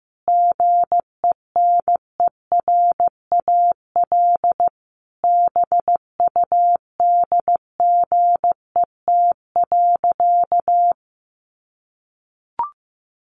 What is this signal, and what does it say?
Morse code, heard as a keyed tone, about 15 wpm, 700 Hz. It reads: GENERAL BUDGET.
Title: Sentences from Top 1200 Words 15wpm